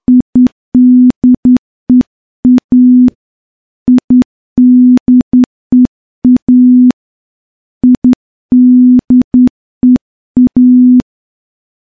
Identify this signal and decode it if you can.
Morse code, heard as a keyed tone, about 10 wpm, 250 Hz. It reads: IDEA IDEA IDEA
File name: morse-training-10WPM